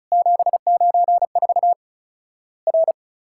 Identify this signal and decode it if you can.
Morse code, heard as a keyed tone, about 35 wpm, 700 Hz. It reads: 794 R
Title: Worked All Europe DX Contest - QTC Exchange - 35wpm